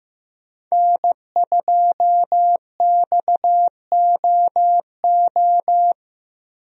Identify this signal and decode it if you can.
Morse code, heard as a keyed tone, about 15 wpm, 700 Hz. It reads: N2XOO
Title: USA Call Signs - 5 Characters - 15wpm